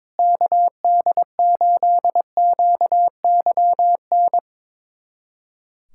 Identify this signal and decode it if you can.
Morse code, heard as a keyed tone, about 22 wpm, 700 Hz. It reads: KD8QYN
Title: USA Callsign DE Callsign - 6 Characters - 22wpm